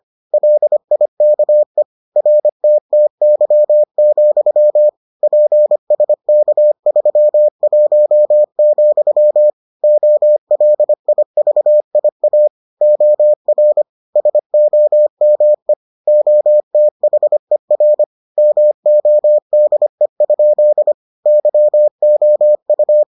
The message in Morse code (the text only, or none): LIKE RTTY, PSK31, OLIVIA OR SOME OTHER MODE? YOU